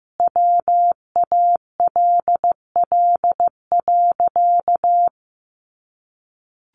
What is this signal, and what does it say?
Morse code, heard as a keyed tone, about 15 wpm, 700 Hz. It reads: WALL.
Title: Sentences from Top 1100 Words 15wpm